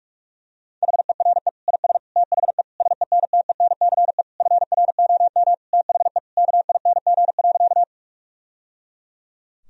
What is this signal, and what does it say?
Morse code, heard as a keyed tone, about 46 wpm, 700 Hz. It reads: HERE IS THE SENTENCE FROM THE KING.
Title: Sentences from Top 1000 Words - Review Part 1 - 45wpm